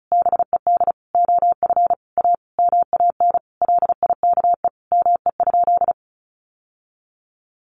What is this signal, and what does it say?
Morse code, heard as a keyed tone, about 34 wpm, 700 Hz. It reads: BED OF A MAN LIKE ME?
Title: Sentences from Top 1200 Words 35wpm